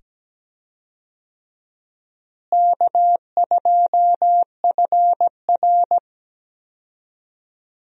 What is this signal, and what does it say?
Morse code, heard as a keyed tone, about 17 wpm, 700 Hz. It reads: K2FR
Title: USA Call Signs - 4 Characters - Repeated 2x - 17wpm